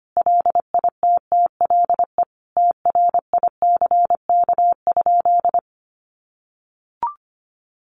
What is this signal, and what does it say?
Morse code, heard as a keyed tone, about 25 wpm, 700 Hz. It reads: LITTLE TRICK?
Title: Sentences from Top 1500 Words 25wpm